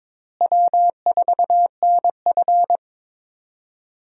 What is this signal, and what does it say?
Morse code, heard as a keyed tone, about 23 wpm, 700 Hz. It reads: W4NF
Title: USA Call Signs - 4 Characters - Repeated 2x - 22wpm